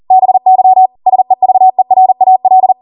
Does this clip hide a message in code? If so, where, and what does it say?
Yes: Morse code from 0.1 to 2.7 s, heard as a keyed tone, about 40 wpm, 750 Hz. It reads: BY SEVERAL